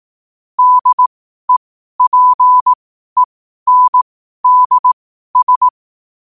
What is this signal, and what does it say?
Morse code, heard as a keyed tone, about 18 wpm, spaced as for 12 wpm, 1000 Hz. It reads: DEPENDS
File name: qotdfiles / 12wpm